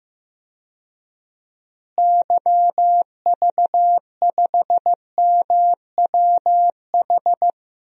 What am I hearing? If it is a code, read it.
Morse code, heard as a keyed tone, about 16 wpm, 700 Hz. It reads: YV5MWH
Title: DXCC-100 Call Signs - 6 Characters - Repeated 2x - 15wpm